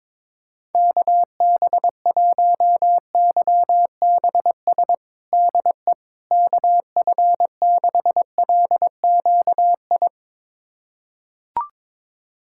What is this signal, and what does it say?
Morse code, heard as a keyed tone, about 22 wpm, 700 Hz. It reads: KB1YBS DE KF6LQI
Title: USA Callsign DE Callsign - 6 Characters - 22wpm